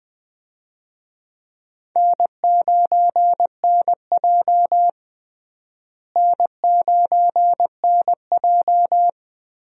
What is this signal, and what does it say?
Morse code, heard as a keyed tone, about 20 wpm, 700 Hz. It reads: N9NJ N9NJ
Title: USA Call Signs - 4 Characters - Repeated 2x - 20wpm